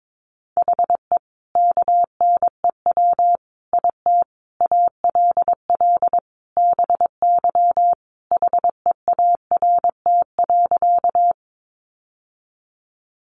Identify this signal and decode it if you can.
Morse code, heard as a keyed tone, about 22 wpm, 700 Hz. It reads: HE KNEW IT ALL BY HEART.
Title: Sentences from Top 1200 Words 22wpm